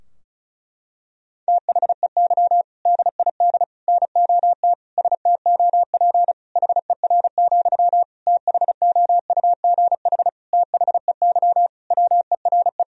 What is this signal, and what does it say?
Morse code, heard as a keyed tone, about 36 wpm, 700 Hz. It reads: THEY DID NOT STOP HER, THOUGH THEY WERE
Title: Sentences from Top 1000 Words - Review Part 2 - 35wpm